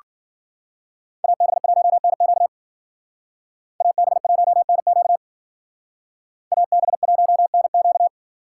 Morse code, heard as a keyed tone, about 49 wpm, 700 Hz. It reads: AB1NQ AB1NQ AB1NQ
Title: Speed Racing - USA 2x2 Call Signs 40wpm